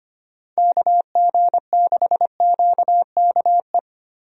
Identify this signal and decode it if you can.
Morse code, heard as a keyed tone, about 25 wpm, 700 Hz. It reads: KG6QKE
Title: USA Call Signs - 6 Characters - Repeated 2x - 25wpm